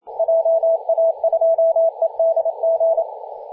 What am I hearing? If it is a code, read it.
Morse code, heard as a keyed tone, about 28 wpm, 660 Hz, with radio noise. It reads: JA2EDG
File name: e51wwa-10-104mhz-cw